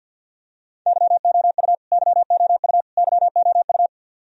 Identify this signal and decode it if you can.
Morse code, heard as a keyed tone, about 51 wpm, 700 Hz. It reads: YOU YOU YOU
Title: Top 100 Words Repeated 50wpm